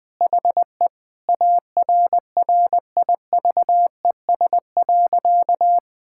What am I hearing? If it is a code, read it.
Morse code, heard as a keyed tone, about 20 wpm, 700 Hz. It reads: HE ARRIVES.
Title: Sentences with 2-Word Phrases - 20wpm